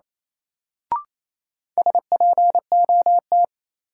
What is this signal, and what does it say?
Morse code, heard as a keyed tone, about 28 wpm, 700 Hz. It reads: SPOT
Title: Sentences from Top 1100 Words - New Word Drill Alt - 28wpm